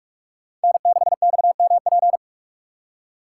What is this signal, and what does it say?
Morse code, heard as a keyed tone, about 46 wpm, 700 Hz. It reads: N6XMP
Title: USA Call Signs - 5 Characters - 3x Character Spacing - 45wpm